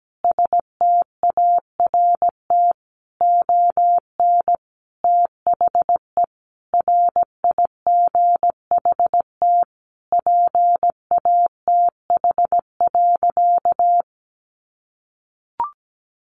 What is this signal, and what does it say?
Morse code, heard as a keyed tone, about 17 wpm, 700 Hz. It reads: START ON THE RIGHT PATH.
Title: Sentences from Top 1400 Words 17wpm